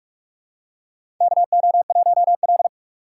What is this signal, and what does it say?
Morse code, heard as a keyed tone, about 47 wpm, 700 Hz. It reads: KO1L
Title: USA Call Signs - 4 Characters - 3x Character Spacing - 45wpm